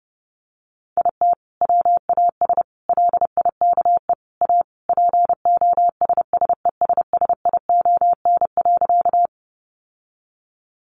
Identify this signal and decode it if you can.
Morse code, heard as a keyed tone, about 30 wpm, 700 Hz. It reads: IT WAS LIKE A POSSESSION.
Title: Sentences from Top 1400 Words 30wpm